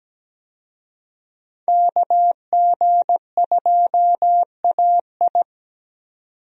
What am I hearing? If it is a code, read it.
Morse code, heard as a keyed tone, about 17 wpm, 700 Hz. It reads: KG2AI
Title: USA Call Signs - 5 Characters - 17wpm